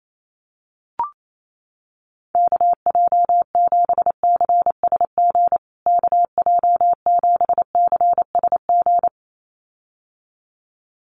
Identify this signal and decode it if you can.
Morse code, heard as a keyed tone, about 28 wpm, 700 Hz. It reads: KJ7CSG KJ7CSG
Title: USA Call Signs Repeated 2x 28wpm